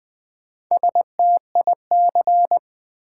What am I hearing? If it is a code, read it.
Morse code, heard as a keyed tone, about 20 wpm, 700 Hz. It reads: STIC
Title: Build A Word - 1 Letter At A Time - 5-Letter Words - 20wpm